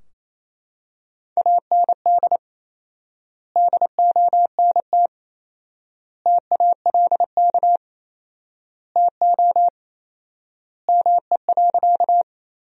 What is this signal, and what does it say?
Morse code, heard as a keyed tone, about 28 wpm, 700 Hz. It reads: AND DONT TALK TO ME.
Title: 4x Word-Spacing - Sentences from Top 400 Words - 28wpm